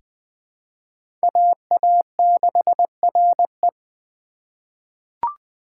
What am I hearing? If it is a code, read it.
Morse code, heard as a keyed tone, about 20 wpm, 700 Hz. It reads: AA6RE
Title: K1USN SST QSO Practice - 20wpm